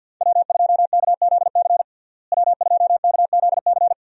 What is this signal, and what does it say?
Morse code, heard as a keyed tone, about 46 wpm, 690 Hz. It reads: W2KZC W2KZC
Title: USA Call Signs - 5 Characters - Repeated 3x - 50wpm